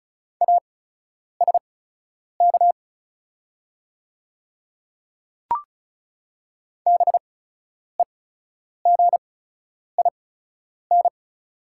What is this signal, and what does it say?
Morse code, heard as a keyed tone, about 37 wpm, spaced as for 10 wpm, 700 Hz. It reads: ASK BEGIN
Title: 8x Character Spacing - Top 300 Words - 35wpm